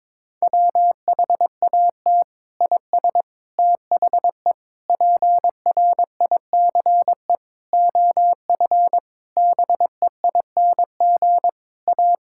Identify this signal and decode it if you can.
Morse code, heard as a keyed tone, about 22 wpm, 700 Hz. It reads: WHAT IS THE PRICE OF BEING A
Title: Sentences from Top 1000 Words - Review Part 5 - 22wpm